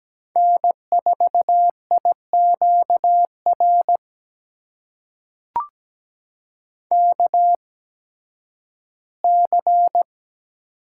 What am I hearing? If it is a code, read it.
Morse code, heard as a keyed tone, about 17 wpm, spaced as for 11 wpm, 700 Hz. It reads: N4IQR KC
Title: USA Call Signs - 8x Character Spacing - 17wpm